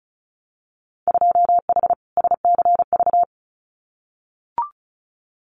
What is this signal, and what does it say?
Morse code, heard as a keyed tone, about 36 wpm, 700 Hz. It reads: 2H SCV
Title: Winter Field Day Exchanges Encore 35wpm